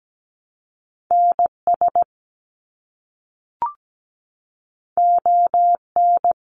Morse code, heard as a keyed tone, about 17 wpm, 700 Hz. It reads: NS ON
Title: Canadian Provinces and Territories - Repeated 3x - 17wpm